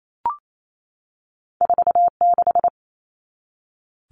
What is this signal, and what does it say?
Morse code, heard as a keyed tone, about 28 wpm, 700 Hz. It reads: 46
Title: All Cut Numbers - 2-Digit 28wpm